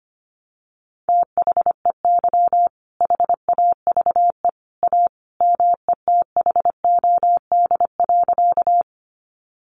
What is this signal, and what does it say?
Morse code, heard as a keyed tone, about 25 wpm, 700 Hz. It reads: THEY HAVE A METHOD.
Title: Sentences from Top 600 Words 25wpm